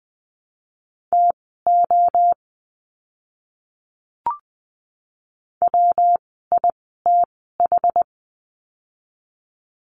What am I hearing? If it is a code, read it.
Morse code, heard as a keyed tone, about 20 wpm, spaced as for 14 wpm, 700 Hz. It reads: TO WITH
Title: 192 Lesson 31f - Letter v - Words 20wpm